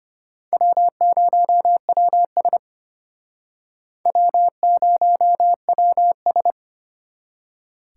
Morse code, heard as a keyed tone, about 33 wpm, 700 Hz. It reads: W0WS W0WS